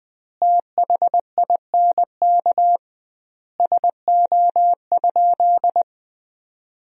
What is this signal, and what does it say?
Morse code, heard as a keyed tone, about 20 wpm, 700 Hz. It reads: THINK SO?
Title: 2x Word-Spacing - Sentences from Top 200 Words - 20wpm